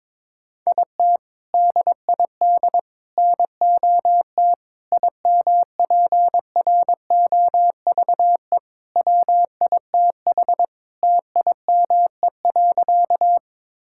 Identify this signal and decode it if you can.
Morse code, heard as a keyed tone, about 22 wpm, 700 Hz. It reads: IT DID NOT IMPROVE WITH TIME.
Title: Sentences from Top 1000 Words - Review Part 5 - 22wpm